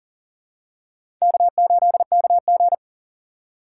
Morse code, heard as a keyed tone, about 39 wpm, 700 Hz. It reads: K8KG
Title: USA Call Signs - 4 Characters - 3x Character Spacing - 40wpm